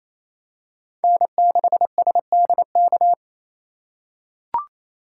Morse code, heard as a keyed tone, about 28 wpm, 700 Hz. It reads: N6SDK
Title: USA Call Signs - 5 Characters - 4x Character Spacing - 28wpm